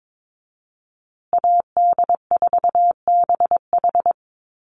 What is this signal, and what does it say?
Morse code, heard as a keyed tone, about 22 wpm, 700 Hz. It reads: AD4BH
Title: USA Call Signs Repeated 2x 22wpm